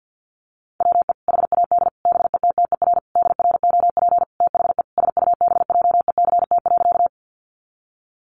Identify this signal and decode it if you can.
Morse code, heard as a keyed tone, about 50 wpm, 700 Hz. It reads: WE HAD BETTER DROP THE SUBJECT.
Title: Sentences from Top 1400 Words 50wpm